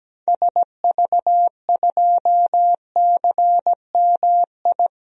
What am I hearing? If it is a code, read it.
Morse code, heard as a keyed tone, about 17 wpm, 700 Hz. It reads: SV2CMI
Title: DXCC-100 Call Signs - 6 Characters - 17wpm